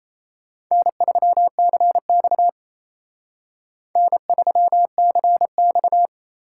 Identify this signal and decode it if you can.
Morse code, heard as a keyed tone, about 34 wpm, 700 Hz. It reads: N3CX N3CX
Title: Speed Racing - USA 1x2 Call Signs 28wpm